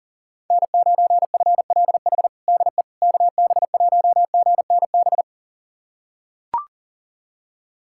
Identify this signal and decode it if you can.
Morse code, heard as a keyed tone, about 40 wpm, 700 Hz. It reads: N9FLH DE KB1GNB
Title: USA Callsign DE Callsign 40wpm